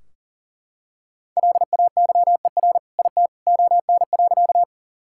Morse code, heard as a keyed tone, about 40 wpm, 700 Hz. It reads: LAYER IT ON.
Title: Sentences from Top 1600 Words 40wpm